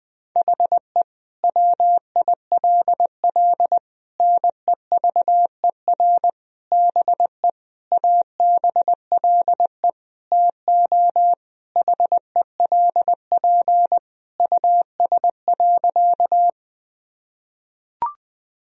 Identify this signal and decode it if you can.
Morse code, heard as a keyed tone, about 21 wpm, 700 Hz. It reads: HE WILL NEVER BE ABLE TO HELP US.
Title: Sentences from Top 1000 Words - Review Part 2 - 20wpm